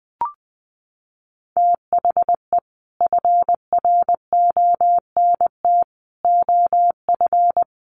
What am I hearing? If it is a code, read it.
Morse code, heard as a keyed tone, about 20 wpm, 700 Hz. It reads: THE FRONT OF
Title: Sets of 4 Words 20wpm